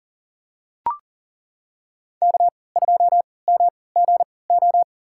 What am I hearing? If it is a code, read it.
Morse code, heard as a keyed tone, about 40 wpm, spaced as for 23 wpm, 700 Hz. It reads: K2MGO
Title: USA Call Signs - 5 Characters - 3x Character Spacing - 40wpm